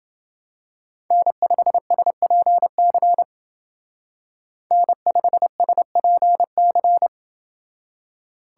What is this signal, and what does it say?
Morse code, heard as a keyed tone, about 29 wpm, 700 Hz. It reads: N5SPC N5SPC